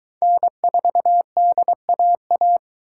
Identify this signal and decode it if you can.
Morse code, heard as a keyed tone, about 22 wpm, 700 Hz. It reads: N4DAA